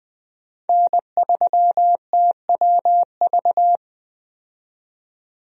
Morse code, heard as a keyed tone, about 20 wpm, 700 Hz. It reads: N3TWV
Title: USA Call Signs - 5 Characters - 20wpm